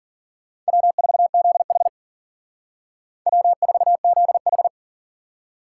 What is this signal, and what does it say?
Morse code, heard as a keyed tone, about 51 wpm, 700 Hz. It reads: W4ZH W4ZH